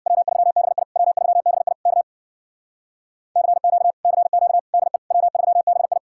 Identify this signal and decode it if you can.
Morse code, heard as a keyed tone, about 66 wpm, 700 Hz. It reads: W3BI W3BI K CQ CQ DE W3BI